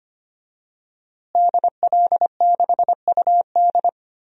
Morse code, heard as a keyed tone, about 25 wpm, 700 Hz. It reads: DL6UD
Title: DXCC-100 Call Signs - 5 Characters - Repeated 3x - 25wpm